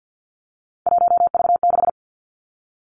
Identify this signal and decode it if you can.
Morse code, heard as a keyed tone, about 50 wpm, 700 Hz. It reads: 1V6
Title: Three Challenging Letter-Number Combinations 50wpm